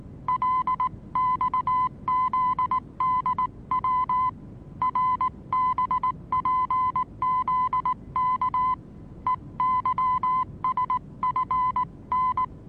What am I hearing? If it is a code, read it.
Morse code, heard as a keyed tone, about 19 wpm, 1010 Hz. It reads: LXZDW RBPZK EYSFN
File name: repos / Morsekurs / DJ4UF Morsen26